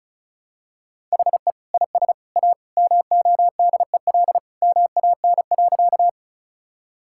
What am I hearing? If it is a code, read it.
Morse code, heard as a keyed tone, about 36 wpm, 700 Hz. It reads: HE IS A MODEL MAN.
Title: Sentences from Top 1000 Words - Review Part 5 - 35wpm